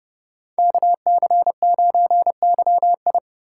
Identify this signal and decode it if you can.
Morse code, heard as a keyed tone, about 39 wpm, 700 Hz. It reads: KC9YI